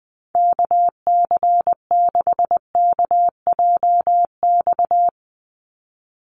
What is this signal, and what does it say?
Morse code, heard as a keyed tone, about 20 wpm, 700 Hz. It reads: KC6KJX
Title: USA Call Signs 20wpm